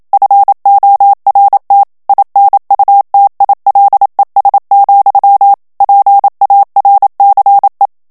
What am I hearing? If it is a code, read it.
Morse code, heard as a keyed tone, about 27 wpm, 800 Hz. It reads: FORT INUTILES, PARCE
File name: FRANCAIS TEXTE CLAIR 2 VITESSE 26 MOTS MINUTES